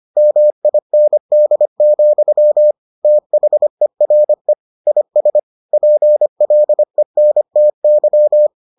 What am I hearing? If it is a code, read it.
Morse code, heard as a keyed tone, about 25 wpm, 600 Hz. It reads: MIND, THERE IS PLENTY